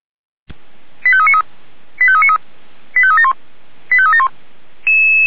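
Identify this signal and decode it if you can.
Five-tone selective call (ZVEI), heard as quick runs of five tones, tones of about 70 ms. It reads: 85393 85393 85371 85371
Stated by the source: ZVEI five-tone call received via SDR, fire brigade Orlamünde (feuerwehr-orlamuende).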